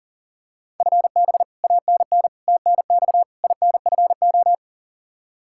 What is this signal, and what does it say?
Morse code, heard as a keyed tone, about 37 wpm, 700 Hz. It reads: FB ANN TNX INFO
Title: Practice QSO - Exchange 3 - More Info - 40wpm